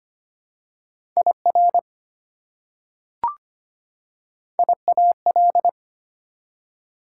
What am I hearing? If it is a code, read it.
Morse code, heard as a keyed tone, about 25 wpm, 690 Hz. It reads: IR IAL
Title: Prefixes and Suffixes 25wpm